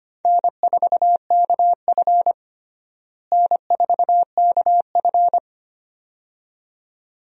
Morse code, heard as a keyed tone, about 25 wpm, 700 Hz. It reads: N4KF N4KF
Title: USA Call Signs - 4 Characters - Repeated 2x - 25wpm